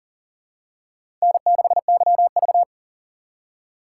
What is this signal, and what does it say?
Morse code, heard as a keyed tone, about 40 wpm, 700 Hz. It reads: N6YV
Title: USA Call Signs - 4 Characters - 4x Character Spacing - 40wpm